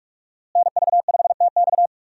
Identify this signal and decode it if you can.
Morse code, heard as a keyed tone, about 46 wpm, 700 Hz. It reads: NV5TX
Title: USA Call Signs - 5 Characters - Repeated 2x - 45wpm